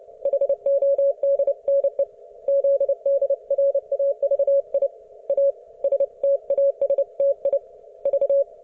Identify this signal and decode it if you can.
Morse code, heard as a keyed tone, about 29 wpm, 560 Hz. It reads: HODNE ZDRAVI A STASTI V